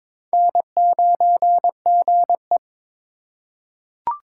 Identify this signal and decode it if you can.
Morse code, heard as a keyed tone, about 22 wpm, 700 Hz. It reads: N9GE
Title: USA Call Signs - 4 Characters - 4x Character Spacing - 22wpm